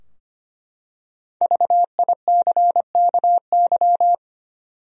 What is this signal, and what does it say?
Morse code, heard as a keyed tone, about 25 wpm, 700 Hz. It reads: VICKY
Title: USA Common Names - Max 5 Letters - 25wpm